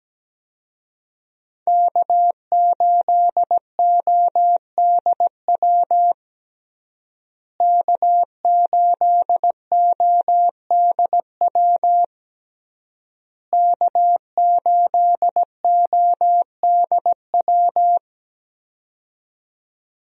Morse code, heard as a keyed tone, about 17 wpm, 700 Hz. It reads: K8ODW K8ODW K8ODW
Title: USA Call Signs - 5 Characters - Repeated 3x - 17wpm